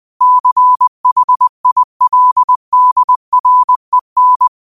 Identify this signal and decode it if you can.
Morse code, heard as a keyed tone, about 20 wpm, 1000 Hz. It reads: CHILDREN
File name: qotdfiles / 20wpm